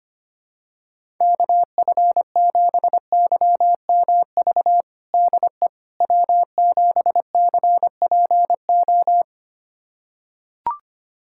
Morse code, heard as a keyed tone, about 25 wpm, 700 Hz. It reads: KF7YMV DE W7CPO
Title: USA Callsign DE Callsign 25wpm